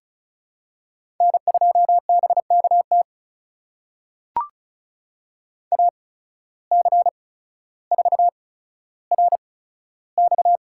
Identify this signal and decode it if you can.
Morse code, heard as a keyed tone, about 34 wpm, spaced as for 15 wpm, 700 Hz. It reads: N2BKT AC4RX
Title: USA Call Signs - 5 Characters - 8x Character Spacing - 35wpm